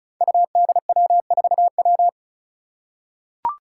Morse code, heard as a keyed tone, about 36 wpm, 700 Hz. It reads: UDW4W
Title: Five Challenging Letter-Number Combinations 35wpm